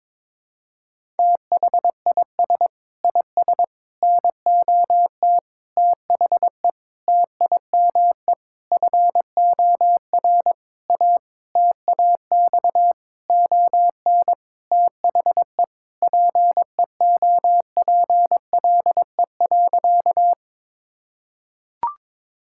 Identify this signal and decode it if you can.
Morse code, heard as a keyed tone, about 21 wpm, 700 Hz. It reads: THIS IS NOT THE TIME FOR A TAX ON THE PEOPLE.
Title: Sentences from Top 1000 Words - Review Part 5 - 22wpm